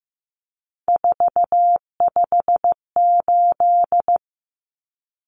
Morse code, heard as a keyed tone, about 15 wpm, 700 Hz. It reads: 458
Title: Numbers - 3-Digit 15wpm